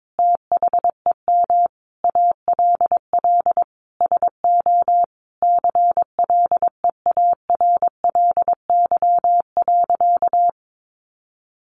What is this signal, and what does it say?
Morse code, heard as a keyed tone, about 22 wpm, 700 Hz. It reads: THEM ALL SO CLEARLY.
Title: Sentences from Top 1300 Words 22wpm